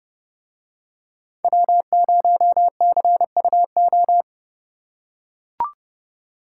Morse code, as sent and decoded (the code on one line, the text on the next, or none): .-- ----- -.-. ..- ---
W0CUO